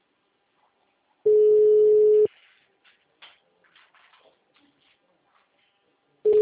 Ringback tone (Europe, 425 Hz).
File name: out- -2016- - - .2989